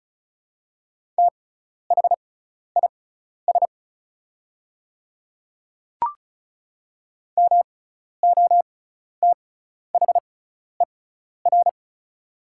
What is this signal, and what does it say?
Morse code, heard as a keyed tone, about 34 wpm, spaced as for 12 wpm, 700 Hz. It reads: THIS MOTHER
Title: 6x Character Spacing - Top 500 Words - 35wpm